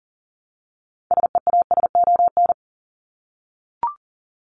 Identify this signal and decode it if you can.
Morse code, heard as a keyed tone, about 37 wpm, 700 Hz. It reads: SEASON
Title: Top 1251-1500 Words 40wpm